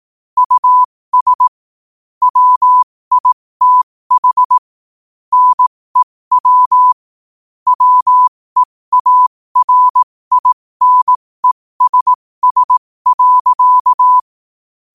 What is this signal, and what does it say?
Morse code, heard as a keyed tone, about 18 wpm, spaced as for 15 wpm, 1000 Hz. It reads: US WITH NEW WEARINESS.
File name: qotdfiles / 15wpm